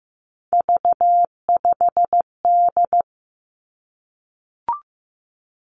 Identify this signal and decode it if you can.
Morse code, heard as a keyed tone, about 15 wpm, 700 Hz. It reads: V5D
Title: Three Challenging Letter-Number Combinations 15wpm